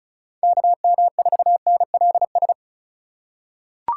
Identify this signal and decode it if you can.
Morse code, heard as a keyed tone, about 36 wpm, 700 Hz. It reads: KM4NLS
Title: USA Call Signs - 6 Characters - 6x Character Spacing - 35wpm